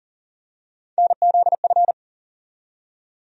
Morse code, heard as a keyed tone, about 39 wpm, 700 Hz. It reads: NZF
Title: USA Call Sign Postfixes - 3 Letters 40wpm